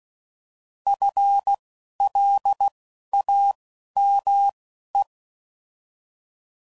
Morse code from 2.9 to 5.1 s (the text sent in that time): A M E